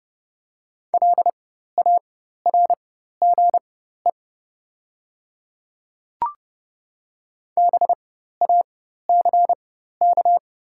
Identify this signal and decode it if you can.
Morse code, heard as a keyed tone, about 30 wpm, spaced as for 14 wpm, 690 Hz. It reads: LARGE BACK
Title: 4x Character Spacing - Top 300 Words - 30wpm